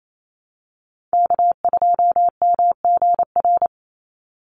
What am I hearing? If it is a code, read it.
Morse code, heard as a keyed tone, about 28 wpm, 700 Hz. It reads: K2MGR
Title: Call Signs - 28wpm